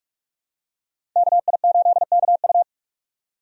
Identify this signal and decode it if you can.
Morse code, heard as a keyed tone, about 46 wpm, 700 Hz. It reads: KI8KU
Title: USA Call Signs - 5 Characters - 45wpm